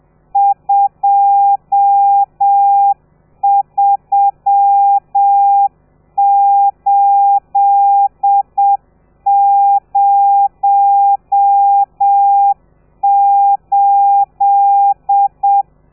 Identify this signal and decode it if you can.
Morse code, heard as a keyed tone, about 7 wpm, 800 Hz. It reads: 23808
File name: telegrafia / i2tfj / Lez12 30